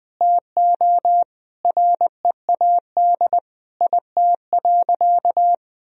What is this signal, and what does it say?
Morse code, heard as a keyed tone, about 20 wpm, 700 Hz. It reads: TO READ IT.
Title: Sentences from Top 500 Words and Abbreviations 20wpm